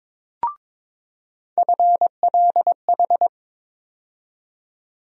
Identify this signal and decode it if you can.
Morse code, heard as a keyed tone, about 22 wpm, 700 Hz. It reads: FLH
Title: USA Call Sign Postfixes - 3 Letters 22wpm